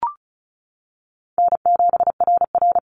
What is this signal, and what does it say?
Morse code, heard as a keyed tone, about 34 wpm, 700 Hz. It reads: N7RR
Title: NA SOTA Activator Callsigns 35wpm